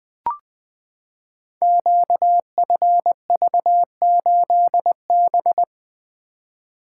Morse code, heard as a keyed tone, about 20 wpm, 700 Hz. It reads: QFV8B
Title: Code Groups - 5 Letter:Numbers - 20wpm